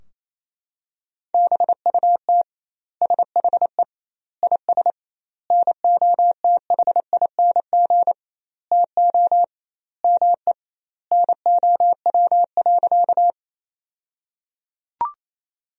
Morse code, heard as a keyed tone, about 28 wpm, 700 Hz. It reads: BUT SHE IS NOTHING TO ME NOW.
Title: 2x Word-Spacing - Sentences from Top 500 Words - 28wpm